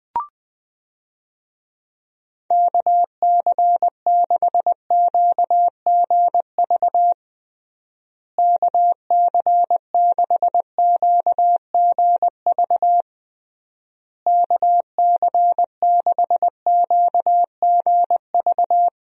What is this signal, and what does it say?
Morse code, heard as a keyed tone, about 21 wpm, 700 Hz. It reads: KC6QGV KC6QGV KC6QGV
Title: USA Call Signs - 6 Characters - Repeated 3x - 20wpm